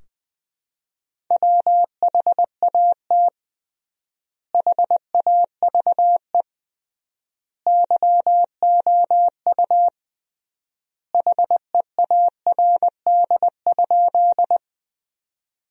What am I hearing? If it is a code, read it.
Morse code, heard as a keyed tone, about 20 wpm, 700 Hz. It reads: WHAT HAVE YOU HEARD?
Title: 3x Word-Spacing - Sentences from Top 400 Words - 20wpm